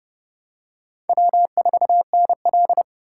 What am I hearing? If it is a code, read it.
Morse code, heard as a keyed tone, about 32 wpm, 700 Hz. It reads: W4NL
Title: USA Call Signs - 4 Characters - 8x Character Spacing - 30wpm